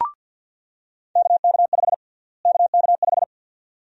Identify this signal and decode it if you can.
Morse code, heard as a keyed tone, about 48 wpm, 700 Hz. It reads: KK5 KK5
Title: USA Call Sign Prefixes - Repeated 2x - 50wpm